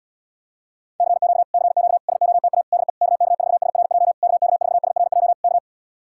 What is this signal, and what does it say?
Morse code, heard as a keyed tone, about 77 wpm, 700 Hz. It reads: CQ CQ SOTA DE KK4SRJ KK4SRJ K